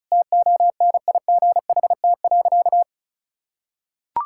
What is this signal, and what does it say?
Morse code, heard as a keyed tone, about 36 wpm, 700 Hz. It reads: TONIGHT.
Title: Sentences from Top 1000 Words - Review Part 5 - 35wpm